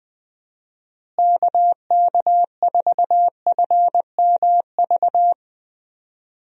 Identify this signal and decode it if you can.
Morse code, heard as a keyed tone, about 20 wpm, 700 Hz. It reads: KK4FMV
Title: USA Call Signs - 6 Characters - Repeated 2x - 20wpm